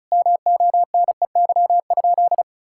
Morse code, heard as a keyed tone, about 36 wpm, 700 Hz. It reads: MONEY?
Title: Sentences from Top 1000 Words - Review Part 5 - 35wpm